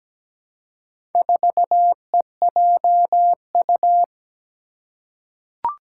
Morse code, heard as a keyed tone, about 17 wpm, 700 Hz. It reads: 4EJU
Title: Four Challenging Letter-Number Combinations 17wpm